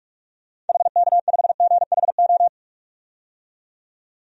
Morse code, heard as a keyed tone, about 47 wpm, 700 Hz. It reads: HK5GHO
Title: DXCC-100 Call Signs - 6 Characters - 45wpm